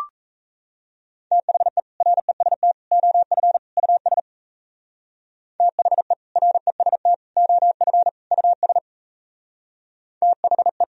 Morse code, heard as a keyed tone, about 34 wpm, 700 Hz. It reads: THE REST OF US THE REST OF US THE
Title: Speed-Racing - Sets of 4 Words 28wpm